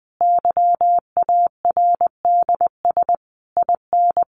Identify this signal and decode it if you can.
Morse code, heard as a keyed tone, about 20 wpm, 700 Hz. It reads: YARDS IN
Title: A Sound Of Thunder 20wpm